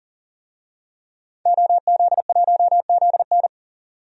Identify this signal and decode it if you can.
Morse code, heard as a keyed tone, about 40 wpm, 700 Hz. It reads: OZ1ZN
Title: DXCC-100 Call Signs - 5 Characters - 40wpm